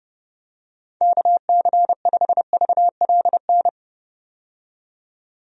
Morse code, heard as a keyed tone, about 29 wpm, 700 Hz. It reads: KC5VLN
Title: USA Call Signs - 6 Characters - 30wpm